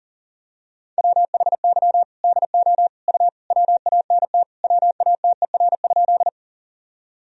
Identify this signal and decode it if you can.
Morse code, heard as a keyed tone, about 40 wpm, 700 Hz. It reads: WHY DO U WANT WATER?
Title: Sentences from Top 200 Words and Abbreviations 40wpm